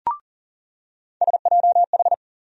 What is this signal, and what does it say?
Morse code, heard as a keyed tone, about 40 wpm, 700 Hz. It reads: SJH
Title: USA Call Sign Postfixes - 3 Letters 40wpm